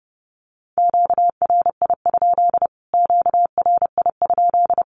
Morse code, heard as a keyed tone, about 30 wpm, 700 Hz. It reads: QRI? QRI?
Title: Q-Codes Repeated 3x 30wpm